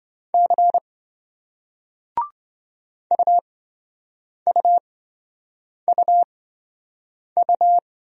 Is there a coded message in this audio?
Yes: Morse code from 0.3 to 7.8 s, heard as a keyed tone, about 27 wpm, spaced as for 7 wpm, 700 Hz. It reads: C UUUU